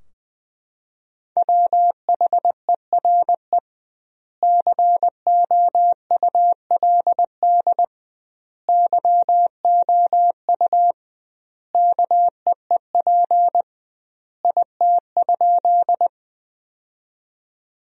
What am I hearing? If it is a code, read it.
Morse code, heard as a keyed tone, about 20 wpm, 700 Hz. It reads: WHERE COULD YOU KEEP IT?
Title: 2x Word-Spacing - Sentences from Top 300 Words - 20wpm